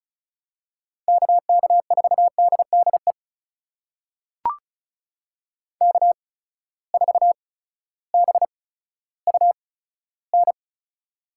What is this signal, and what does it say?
Morse code, heard as a keyed tone, about 36 wpm, spaced as for 16 wpm, 700 Hz. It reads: KK4DDE K4BUN
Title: USA Call Signs - 8x Character Spacing - 35wpm